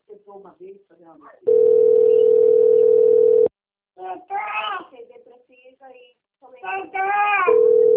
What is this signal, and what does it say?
Ringback tone (North America), 440 + 480 Hz, with speech.
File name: out- - - - - .36397